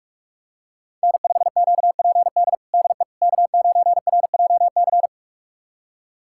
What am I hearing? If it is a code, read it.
Morse code, heard as a keyed tone, about 41 wpm, 700 Hz. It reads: N5QPD DE K9RJC
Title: USA Callsign DE Callsign - 5 Characters - 45wpm